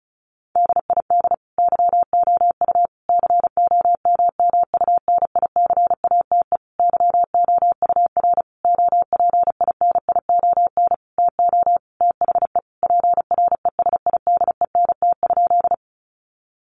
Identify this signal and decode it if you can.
Morse code, heard as a keyed tone, about 34 wpm, 700 Hz. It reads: DID YOU COMMUNICATE YOUR OPINION TO THE PRESIDENT?
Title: Sentences from Top 1300 Words 35wpm